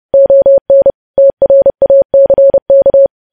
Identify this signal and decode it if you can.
Morse code, heard as a keyed tone, about 30 wpm, 550 Hz. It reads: ON TRACK
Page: news headlines in morse code at 30 wpm